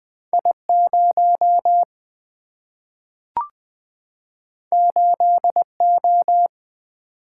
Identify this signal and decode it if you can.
Morse code, heard as a keyed tone, about 21 wpm, 700 Hz. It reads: I 0 8 O
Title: Single-Letter Single-Number 20wpm